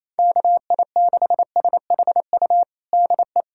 Morse code, heard as a keyed tone, about 28 wpm, 700 Hz. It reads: KI6SHU DE
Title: USA Callsign DE Callsign - 6 Characters - 28wpm